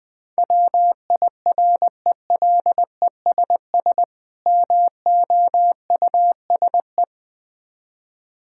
Morse code, heard as a keyed tone, about 20 wpm, 700 Hz. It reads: WIRELESS MOUSE
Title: Difficult Words with Hints - 20wpm